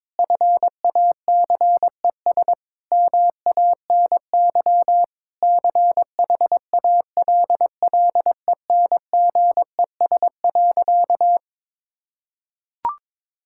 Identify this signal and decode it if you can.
Morse code, heard as a keyed tone, about 22 wpm, 700 Hz. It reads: FACES MANY CHALLENGES.
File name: Sentences with 7-Word Phrases - 22wpm